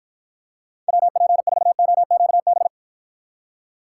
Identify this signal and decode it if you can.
Morse code, heard as a keyed tone, about 53 wpm, 700 Hz. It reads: WP4OQB